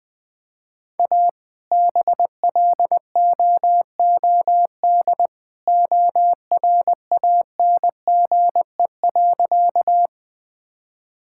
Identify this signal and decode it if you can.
Morse code, heard as a keyed tone, about 20 wpm, 700 Hz. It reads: A BLOOD ORANGE.
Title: All-Age Appropriate Jokes - 20wpm